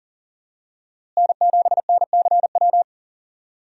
Morse code, heard as a keyed tone, about 40 wpm, 700 Hz. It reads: N7NCW
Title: USA Call Signs - 5 Characters - Repeated 2x - 40wpm